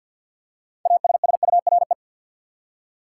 Morse code, heard as a keyed tone, about 48 wpm, 700 Hz. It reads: ASSURE
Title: Six Letter Words - 50wpm